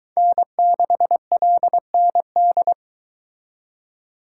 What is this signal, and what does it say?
Morse code, heard as a keyed tone, about 22 wpm, 700 Hz. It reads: N6LND